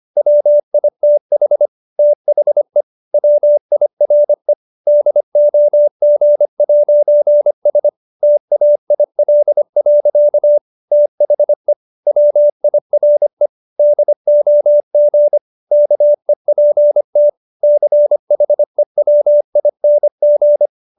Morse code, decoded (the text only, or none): WITH THE WIRE DOG'S TAIL. THE WIRE DOG KEPT CHEWING